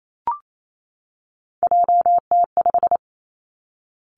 Morse code, heard as a keyed tone, about 28 wpm, 700 Hz. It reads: JT5
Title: Three Challenging Letter-Number Combinations 28wpm